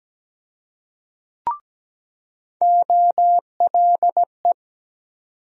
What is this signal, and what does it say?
Morse code, heard as a keyed tone, about 17 wpm, 700 Hz. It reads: OLE
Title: USA Call Sign Postfixes - 3 Letters 17wpm